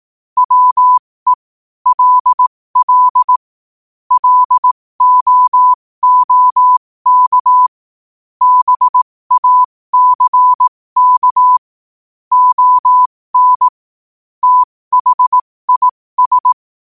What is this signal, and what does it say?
Morse code, heard as a keyed tone, about 18 wpm, spaced as for 15 wpm, 1000 Hz. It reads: WE LL LOOK BACK ON THIS
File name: 15wpm